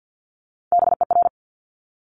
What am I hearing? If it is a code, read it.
Morse code, heard as a keyed tone, about 50 wpm, 700 Hz. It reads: BER
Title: Top Three Letter Combinations in Words 50wpm